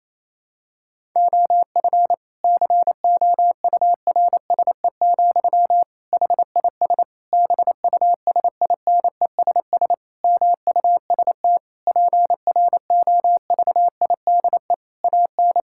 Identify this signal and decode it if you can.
Morse code, heard as a keyed tone, about 28 wpm, 700 Hz. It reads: OF COURSE, HIS BUSINESS MUST PROVIDE AN
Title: Sentences from Top 1000 Words - Review Part 5 - 28wpm